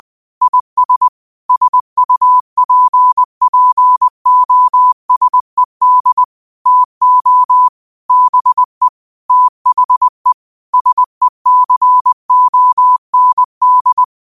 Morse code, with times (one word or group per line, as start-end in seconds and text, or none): 0.4-1.1 IS
1.5-6.2 SUPPOSED
6.7-7.7 TO
8.1-8.9 BE
9.3-10.3 THE
10.7-14.0 SECOND